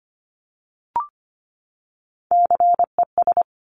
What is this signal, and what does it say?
Morse code, heard as a keyed tone, about 25 wpm, 700 Hz. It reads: CES
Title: Top Three Letter Combinations in Words 25wpm